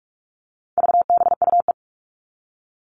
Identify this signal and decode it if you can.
Morse code, heard as a keyed tone, about 44 wpm, 700 Hz. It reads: VBUE
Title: All Cut Numbers - 4-Digit 45wpm